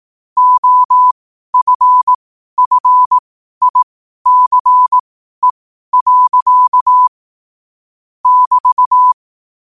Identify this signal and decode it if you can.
Morse code, heard as a keyed tone, about 18 wpm, spaced as for 12 wpm, 1000 Hz. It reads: OFFICE. <BT>
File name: qotdfiles / 12wpm